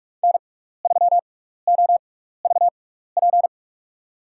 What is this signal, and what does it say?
Morse code, heard as a keyed tone, about 46 wpm, spaced as for 15 wpm, 700 Hz. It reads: N3OVP
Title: USA Call Signs - 6x Character Spacing - 45wpm